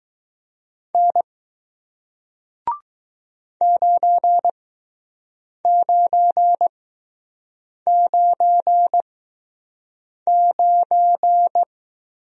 Morse code, heard as a keyed tone, about 20 wpm, spaced as for 6 wpm, 700 Hz. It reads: N 9999